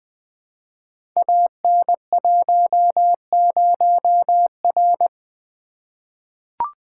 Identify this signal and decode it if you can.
Morse code, heard as a keyed tone, about 20 wpm, 700 Hz. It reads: AN10R
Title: Code Groups - 5 Letter:Numbers - 20wpm